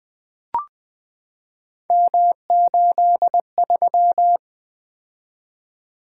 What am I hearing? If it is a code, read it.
Morse code, heard as a keyed tone, about 20 wpm, 700 Hz. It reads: M83
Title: Three Letter-Number Combinations - 20wpm